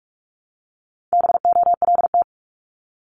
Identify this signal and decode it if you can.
Morse code, heard as a keyed tone, about 46 wpm, 700 Hz. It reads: BOLT
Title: Multi-letter Phonemes Repeated 3x 45wpm